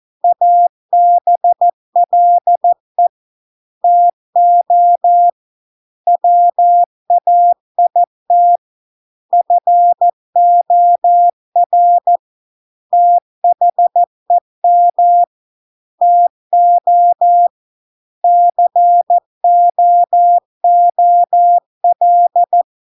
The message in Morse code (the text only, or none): ABLE TO WAIT FOR THEM TO COOL